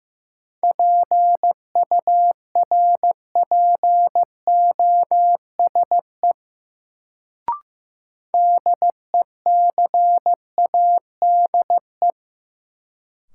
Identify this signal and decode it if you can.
Morse code, heard as a keyed tone, about 15 wpm, 700 Hz. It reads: PURPOSE DECADE
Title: Sets of 5 Words - New Word Drill - 15wpm